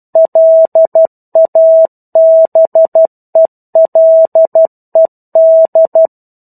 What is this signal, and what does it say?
Morse code, heard as a keyed tone, about 12 wpm, 650 Hz. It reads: LABELED
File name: 01-Chapter-01 12 WPM